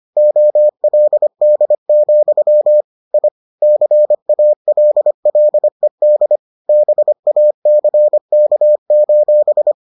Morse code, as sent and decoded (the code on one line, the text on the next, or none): --- .-.. -.. --..-- / .. / -.-. .- .-.. .-.. -.. / -... .- -.-. -.- ---...
OLD, I CALLED BACK: